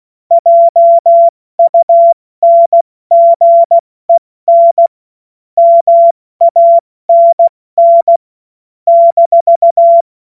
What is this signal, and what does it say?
Morse code, heard as a keyed tone, about 16 wpm, 680 Hz. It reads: JUNGEN MANN -
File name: Morsekurs / Witze / Witz 06. Dezember 2022 70 F80